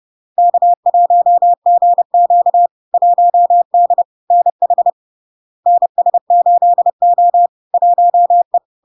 Morse code, heard as a keyed tone, about 30 wpm, 700 Hz. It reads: K1GQ 1D NH NS8O 1E
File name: LARK-FD-2022-Log-in-CW-30wpm-700-Hz